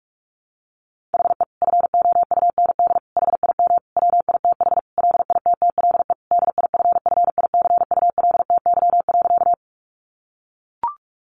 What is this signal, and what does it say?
Morse code, heard as a keyed tone, about 44 wpm, 700 Hz. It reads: HE FOUND HIM WITH LITTLE DIFFICULTY.
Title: Sentences from Top 1500 Words 45wpm